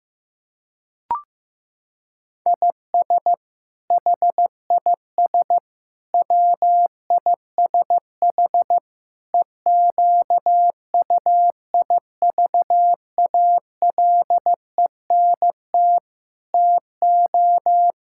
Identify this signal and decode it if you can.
Morse code, heard as a keyed tone, about 15 wpm, 700 Hz. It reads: IS HIS WISH EQUIVALENT TO
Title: Sentences from Top 1600 Words 15wpm